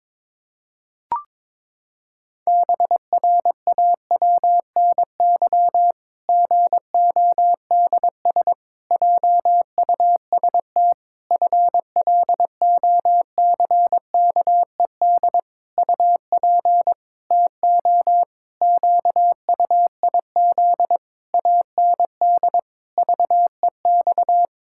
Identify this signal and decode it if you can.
Morse code, heard as a keyed tone, about 22 wpm, 700 Hz. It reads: BRAWNY GODS JUST FLOCKED UP TO QUIZ AND VEX
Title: Pangram Challenge v2 - 22wpm